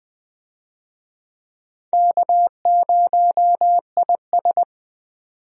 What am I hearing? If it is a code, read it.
Morse code, heard as a keyed tone, about 21 wpm, 700 Hz. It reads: K0IS
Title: USA Call Signs - 4 Characters - 20wpm